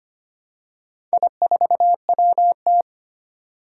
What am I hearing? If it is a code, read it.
Morse code, heard as a keyed tone, about 25 wpm, 700 Hz. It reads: I4WT
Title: Four Challenging Letter-Number Combinations 25wpm